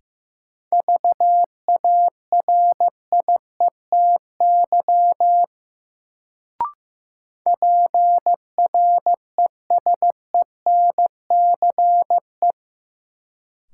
Morse code, heard as a keyed tone, about 15 wpm, 700 Hz. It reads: VARIETY PRESENCE
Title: Sets of 3 Words - New Word Drill - 15wpm